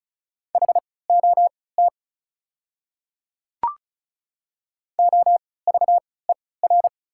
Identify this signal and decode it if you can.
Morse code, heard as a keyed tone, about 36 wpm, spaced as for 20 wpm, 690 Hz. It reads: HOT OVER
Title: 3x Character Spacing - Top 300 Words - 35wpm